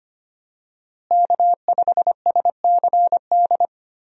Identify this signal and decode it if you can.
Morse code, heard as a keyed tone, about 25 wpm, 700 Hz. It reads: K5SCD
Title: USA Call Signs - 5 Characters - Repeated 2x - 25wpm